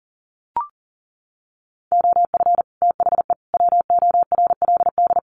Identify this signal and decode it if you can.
Morse code, heard as a keyed tone, about 40 wpm, 700 Hz. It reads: OF THE WORLD
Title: Sets of 3 Words 40wpm